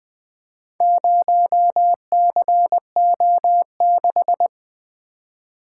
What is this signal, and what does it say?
Morse code, heard as a keyed tone, about 20 wpm, 700 Hz. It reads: 0CO6
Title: Four Letter-Number Combinations - 20wpm